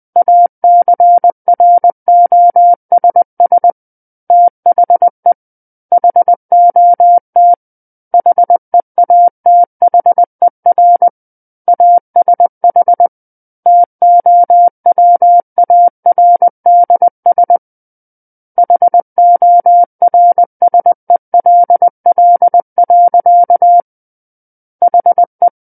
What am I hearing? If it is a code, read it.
Morse code, heard as a keyed tone, about 20 wpm, 700 Hz. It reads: ACROSS THE HOT HEATHER ASH TOWARDS HORSELL. HE
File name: 20 WPM